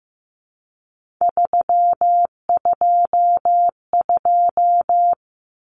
Morse code, heard as a keyed tone, about 15 wpm, 700 Hz. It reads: 322
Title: Numbers - 3-Digit 15wpm